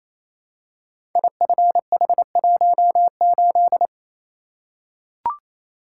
Morse code, heard as a keyed tone, about 27 wpm, 700 Hz. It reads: IFH18
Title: Code Groups - 5 Letter:Numbers - 28wpm